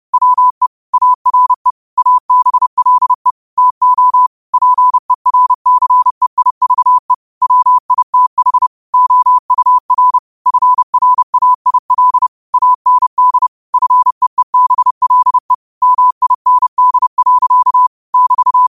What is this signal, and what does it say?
Morse code, heard as a keyed tone, about 30 wpm, 1000 Hz. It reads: WE ARE ABLE TO PERCEIVE WITH OUR FRAIL AND FEEBLE MIND. <BT>